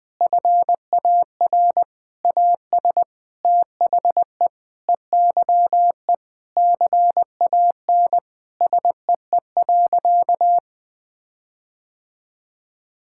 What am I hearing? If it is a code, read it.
Morse code, heard as a keyed tone, about 20 wpm, 700 Hz. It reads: FAR AS THE EYE CAN SEE.
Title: Sentences with 7-Word Phrases - 20wpm